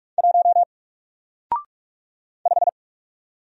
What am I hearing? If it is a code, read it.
Morse code, heard as a keyed tone, about 44 wpm, 700 Hz. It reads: 1 5
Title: Single Letter-Number Alt 45wpm